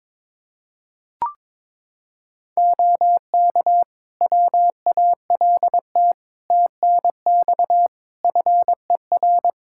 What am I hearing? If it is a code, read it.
Morse code, heard as a keyed tone, about 22 wpm, 700 Hz. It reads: OK WALT TNX FER
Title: Practice QSO - Exchange 4 - Ending - 22wpm